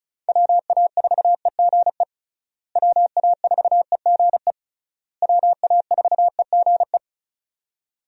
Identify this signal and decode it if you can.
Morse code, heard as a keyed tone, about 34 wpm, 700 Hz. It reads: WA4EGE WA4EGE WA4EGE
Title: USA Call Signs - 6 Characters - Repeated 3x - 35wpm